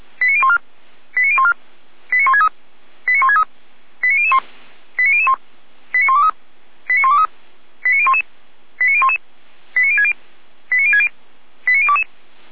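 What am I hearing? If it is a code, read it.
Five-tone selective call (ZVEI), heard as quick runs of five tones, tones of about 70 ms. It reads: 89014 89014 89163 89163 89001 89001 89123 89123 89011 89011 89060 89060 89022
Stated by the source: ZVEI five-tone selective call received on an SDR.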